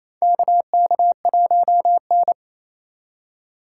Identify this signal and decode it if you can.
Morse code, heard as a keyed tone, about 28 wpm, 700 Hz. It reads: KK1N
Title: USA Call Signs - 3x Character Spacing - 28wpm